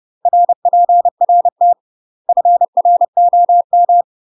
Morse code, heard as a keyed tone, about 30 wpm, 700 Hz. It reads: RPRT FROM
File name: UK MASTER QSOs TRAINING / 30WPM / UK MASTER QSOs 0047